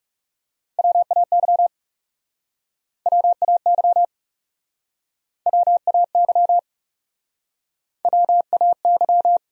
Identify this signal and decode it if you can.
Morse code, heard as a keyed tone, about 40 wpm, 700 Hz. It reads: WAY WAY WAY WAY